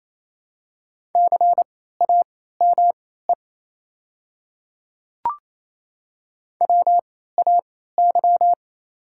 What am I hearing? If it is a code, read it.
Morse code, heard as a keyed tone, about 27 wpm, spaced as for 16 wpm, 690 Hz. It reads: CAME WAY
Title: 3x Character Spacing - Top 200 Words - 28wpm